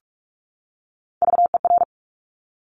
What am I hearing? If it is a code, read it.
Morse code, heard as a keyed tone, about 46 wpm, 700 Hz. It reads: VER
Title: Top Three Letter Combinations in Words 45wpm